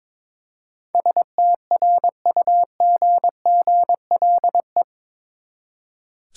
Morse code, heard as a keyed tone, about 22 wpm, 700 Hz. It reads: STRUGGLE
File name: practice / 22wpm / Eight Letter Words - 22wpm